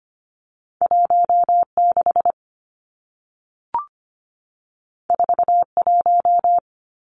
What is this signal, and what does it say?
Morse code, heard as a keyed tone, about 25 wpm, 700 Hz. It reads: 16 41
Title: Numbers - 2-Digit 25wpm